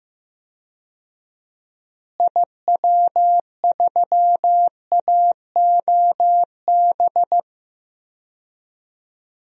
Morse code, heard as a keyed tone, about 15 wpm, 700 Hz. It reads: IW3AOB
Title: DXCC-100 Call Signs - 6 Characters - Repeated 2x - 15wpm